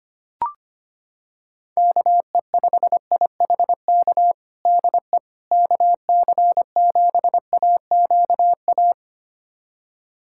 Morse code, heard as a keyed tone, about 25 wpm, 700 Hz. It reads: KE5IHK DE KC7AQA
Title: USA Callsign DE Callsign - 6 Characters - 25wpm